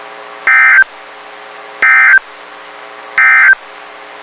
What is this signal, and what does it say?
Emergency Alert System (SAME) tones, heard as harsh buzzing bursts. It is heard from 0.4 to 3.6 s.
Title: WXK91 NOAA Weather Radio